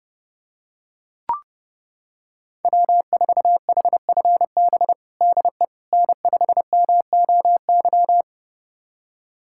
Morse code, heard as a keyed tone, about 30 wpm, 700 Hz. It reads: W4HFB DE N5MOY
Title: USA Callsign DE Callsign - 5 Characters - 30wpm